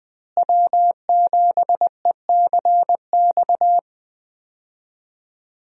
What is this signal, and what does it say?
Morse code, heard as a keyed tone, about 21 wpm, 700 Hz. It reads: W7ECX
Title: USA Call Signs - 5 Characters - Repeated 2x - 20wpm